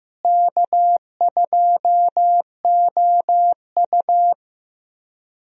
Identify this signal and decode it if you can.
Morse code, heard as a keyed tone, about 16 wpm, 700 Hz. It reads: K2OU
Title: USA Call Signs - 4 Characters - Repeated 2x - 15wpm